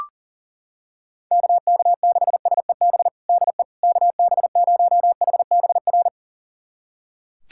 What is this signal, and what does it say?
Morse code, heard as a keyed tone, about 40 wpm, 700 Hz. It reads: KK6SEB DE KB0HBR
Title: USA Callsign DE Callsign - 6 Characters - 40wpm